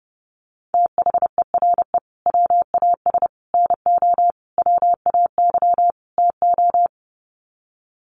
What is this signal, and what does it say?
Morse code, heard as a keyed tone, about 30 wpm, 700 Hz. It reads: THERE WAS NO WAY TO
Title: Sets of 5 Words Encore 30wpm